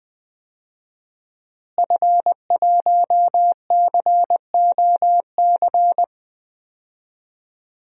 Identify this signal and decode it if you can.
Morse code, heard as a keyed tone, about 20 wpm, 700 Hz. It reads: F1COC
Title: DXCC-100 Call Signs - 5 Characters - Repeated 2x - 20wpm